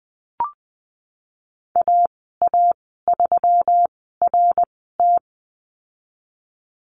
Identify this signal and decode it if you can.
Morse code, heard as a keyed tone, about 19 wpm, spaced as for 14 wpm, 700 Hz. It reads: AA3RT
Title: 167 Lesson 27g - Number 7 - Call Signs 20wpm